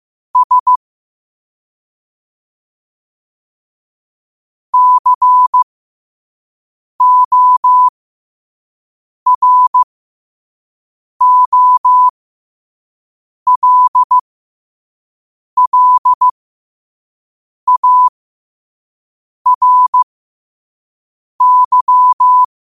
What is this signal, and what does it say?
Morse code, heard as a keyed tone, about 15 wpm, spaced as for 5 wpm, 1000 Hz. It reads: S COROLLARY
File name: qotdfiles / 5wpm